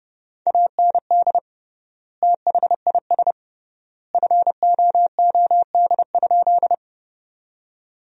Morse code, heard as a keyed tone, about 32 wpm, 700 Hz. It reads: AND THIS FOOD?
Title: 3x Word-Spacing - Sentences from Top 300 Words - 30wpm